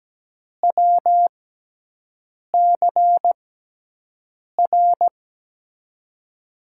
Morse code, heard as a keyed tone, about 17 wpm, spaced as for 6 wpm, 700 Hz. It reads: WCR
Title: USA Call Signs - 6x Character Spacing - 17wpm